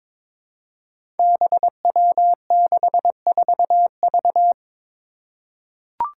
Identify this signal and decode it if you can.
Morse code, heard as a keyed tone, about 22 wpm, 700 Hz. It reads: BW64V
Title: Five Challenging Letter-Number Combinations 22wpm